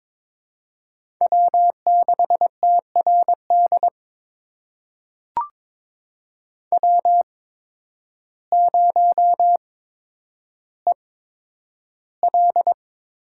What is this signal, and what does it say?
Morse code, heard as a keyed tone, about 22 wpm, spaced as for 10 wpm, 700 Hz. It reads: W6TRD W0EL
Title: USA Call Signs - 8x Character Spacing - 22wpm